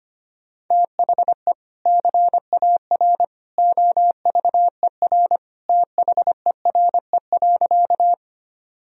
Morse code, heard as a keyed tone, about 25 wpm, 700 Hz. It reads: THE CAR OVER THERE.
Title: Sentences from Top 1000 Words - Review Part 1 - 25wpm